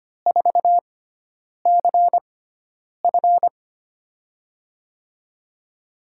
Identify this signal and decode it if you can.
Morse code, heard as a keyed tone, about 25 wpm, spaced as for 9 wpm, 700 Hz. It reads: 4CF
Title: USA Call Signs - 6x Character Spacing - 25wpm